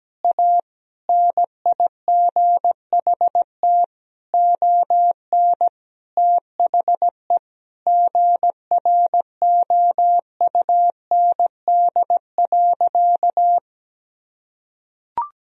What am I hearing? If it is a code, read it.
Morse code, heard as a keyed tone, about 17 wpm, 700 Hz. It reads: A NIGHT ON THE GROUND.
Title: Sentences from Top 1000 Words - Review Part 2 - 17wpm